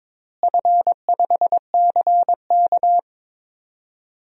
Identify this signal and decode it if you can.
Morse code, heard as a keyed tone, about 22 wpm, 700 Hz. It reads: F5CK
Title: DXCC-100 Call Signs - 4 Characters - Repeated 2x - 22wpm